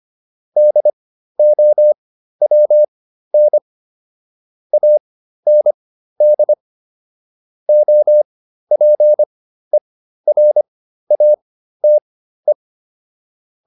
Morse code, heard as a keyed tone, about 25 wpm, spaced as for 13 wpm, 600 Hz. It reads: DOWN AND OPERATE